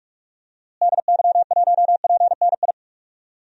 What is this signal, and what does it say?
Morse code, heard as a keyed tone, about 45 wpm, 700 Hz. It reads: DY1PNI
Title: DXCC-100 Call Signs - 6 Characters - Repeated 2x - 45wpm